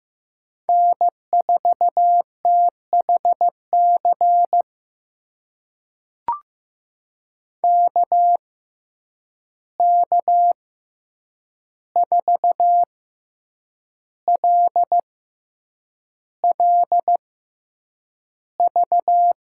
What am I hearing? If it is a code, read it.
Morse code, heard as a keyed tone, about 15 wpm, spaced as for 7 wpm, 700 Hz. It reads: N4THC KK4LLV
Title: USA Call Signs - 6x Character Spacing - 15wpm